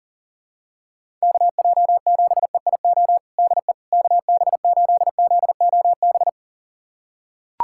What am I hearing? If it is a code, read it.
Morse code, heard as a keyed tone, about 40 wpm, 700 Hz. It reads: KJ7EIO DE KB8ZOB
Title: USA Callsign DE Callsign 40wpm